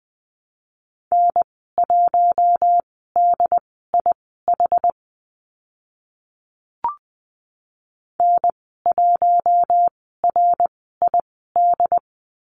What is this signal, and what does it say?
Morse code, heard as a keyed tone, about 20 wpm, spaced as for 14 wpm, 700 Hz. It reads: N1DIH N1RID
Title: 076 Lesson 13g - Letter L - Call Signs 20wpm